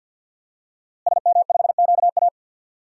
Morse code, heard as a keyed tone, about 48 wpm, 700 Hz. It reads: SM5QA
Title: DXCC-100 Call Signs - 5 Characters - Repeated 2x - 50wpm